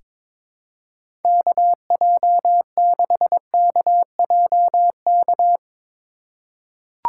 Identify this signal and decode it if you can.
Morse code, heard as a keyed tone, about 21 wpm, 700 Hz. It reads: KJ6KJK
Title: USA Call Signs - 6 Characters - 3x Character Spacing - 22wpm